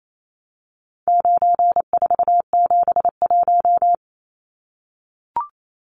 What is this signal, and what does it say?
Morse code, heard as a keyed tone, about 28 wpm, 700 Hz. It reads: 9471
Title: Numbers - 4-Digit 28wpm